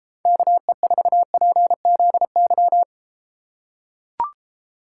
Morse code, heard as a keyed tone, about 33 wpm, 700 Hz. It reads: KE4PZY